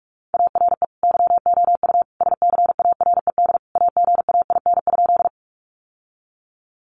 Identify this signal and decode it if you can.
Morse code, heard as a keyed tone, about 44 wpm, 700 Hz. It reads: ARE YOU SCARED AGAIN?
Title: Sentences from Top 700 Words 45wpm